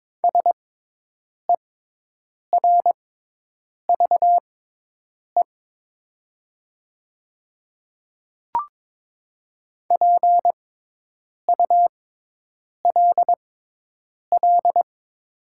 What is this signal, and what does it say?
Morse code, heard as a keyed tone, about 22 wpm, spaced as for 8 wpm, 700 Hz. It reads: SERVE PULL
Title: 6x Character Spacing - Top 500 Words - 22wpm